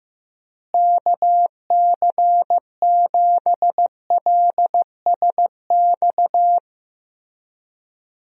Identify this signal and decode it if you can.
Morse code, heard as a keyed tone, about 14 wpm, 700 Hz. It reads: KC7LSX
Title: USA Call Signs - 6 Characters - Repeated 3x - 15wpm